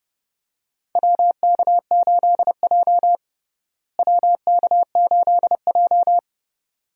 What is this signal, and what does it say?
Morse code, heard as a keyed tone, about 32 wpm, 700 Hz. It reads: WK8J WK8J
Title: USA Call Signs - 4 Characters - Repeated 2x - 30wpm